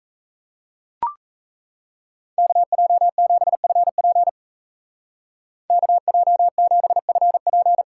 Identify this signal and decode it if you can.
Morse code, heard as a keyed tone, about 34 wpm, 700 Hz. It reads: KJ7FP KJ7FP
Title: Speed Racing - USA 2x2 Call Signs 28wpm